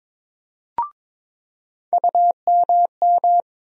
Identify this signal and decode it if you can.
Morse code, heard as a keyed tone, about 22 wpm, 700 Hz. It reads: UMM
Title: USA Call Sign Postfixes - 3 Letters 22wpm